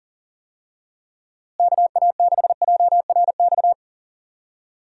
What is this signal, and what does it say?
Morse code, heard as a keyed tone, about 40 wpm, 700 Hz. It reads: KA6JRX
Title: USA Call Signs - 6 Characters - Repeated 3x - 40wpm